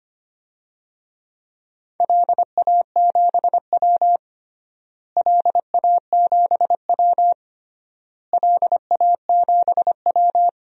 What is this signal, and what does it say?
Morse code, heard as a keyed tone, about 25 wpm, 700 Hz. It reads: LA7W LA7W LA7W
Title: DXCC-100 Call Signs - 4 Characters - Repeated 3x - 25wpm